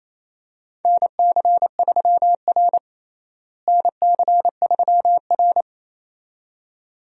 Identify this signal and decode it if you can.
Morse code, heard as a keyed tone, about 28 wpm, 700 Hz. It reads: NC3R NC3R
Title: USA Call Signs - 4 Characters - Repeated 2x - 28wpm